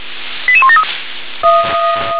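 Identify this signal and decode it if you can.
Five-tone selective call (ZVEI), heard as quick runs of five tones, tones of about 70 ms. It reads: 88173
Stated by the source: ZVEI five-tone selective call received by SDR.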